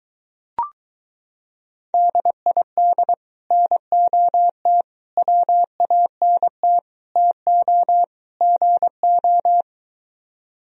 Morse code, heard as a keyed tone, about 23 wpm, 700 Hz. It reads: DID NOT WANT TO GO